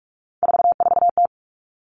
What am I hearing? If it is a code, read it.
Morse code, heard as a keyed tone, about 46 wpm, 700 Hz. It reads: 44T
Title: Three Challenging Letter-Number Combinations 45wpm